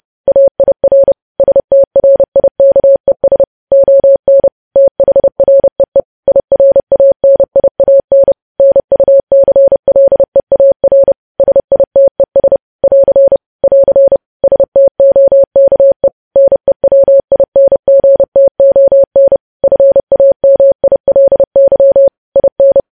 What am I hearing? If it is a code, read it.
Morse code, heard as a keyed tone, about 30 wpm, 550 Hz. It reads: AIR STRIKES ON THREE IRANIAN NUCLEAR SITES <AR> <AR> STOKE NEWINGTON FAMILY IN
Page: news headlines in morse code at 30 wpm